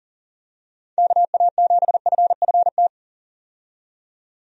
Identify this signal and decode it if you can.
Morse code, heard as a keyed tone, about 40 wpm, 700 Hz. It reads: KA7FFT
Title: USA Call Signs - 6 Characters - 40wpm